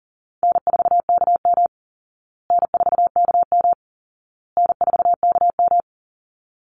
Morse code, heard as a keyed tone, about 40 wpm, 700 Hz. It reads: N4KM N4KM N4KM
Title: Call Signs Repeated 2x - 40wpm